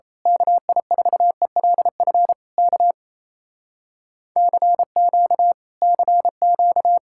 Morse code, heard as a keyed tone, about 26 wpm, 700 Hz. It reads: KI4ELF K CQ CQ
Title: Speed-Racing Common QSO Phrases - 28wpm